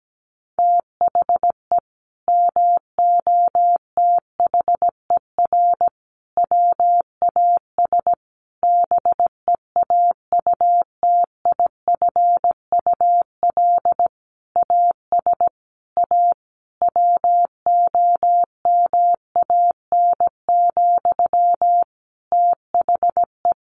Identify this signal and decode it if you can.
Morse code, heard as a keyed tone, about 17 wpm, 700 Hz. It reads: THE MOTHER WAS BEAUTIFUL AS A WOMAN, THE
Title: Sentences from Top 1500 Words 17wpm